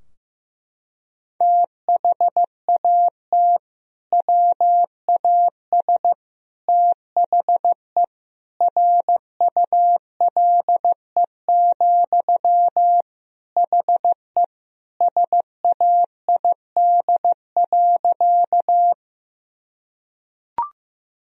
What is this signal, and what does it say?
Morse code, heard as a keyed tone, about 15 wpm, 700 Hz. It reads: THAT WAS THE RULE, HE SAID.
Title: Sentences from Top 1000 Words - Review Part 2 - 15wpm